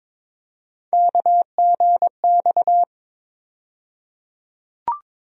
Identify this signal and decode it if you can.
Morse code, heard as a keyed tone, about 22 wpm, 700 Hz. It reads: KGX
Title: USA Call Sign Postfixes - 3 Letters 22wpm